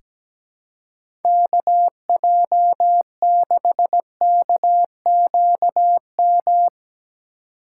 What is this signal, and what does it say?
Morse code, heard as a keyed tone, about 17 wpm, 700 Hz. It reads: KJ6KQM
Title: USA Call Signs - 6 Characters - 3x Character Spacing - 17wpm